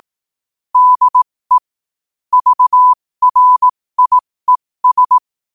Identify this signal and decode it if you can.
Morse code, heard as a keyed tone, about 18 wpm, spaced as for 15 wpm, 1000 Hz. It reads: DE VRIES
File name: qotdfiles / 15wpm